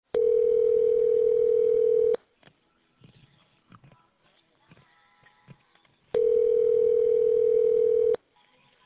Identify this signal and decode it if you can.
Ringback tone (North America), 440 + 480 Hz.